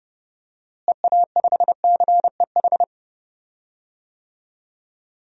Morse code, heard as a keyed tone, about 30 wpm, 700 Hz. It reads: EA5CEH
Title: DXCC-100 Call Signs - 6 Characters - Repeated 2x - 30wpm